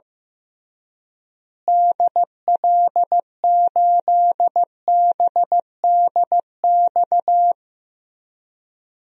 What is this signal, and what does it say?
Morse code, heard as a keyed tone, about 15 wpm, 700 Hz. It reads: DL8BDX
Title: DXCC-100 Call Signs - 6 Characters - Repeated 3x - 15wpm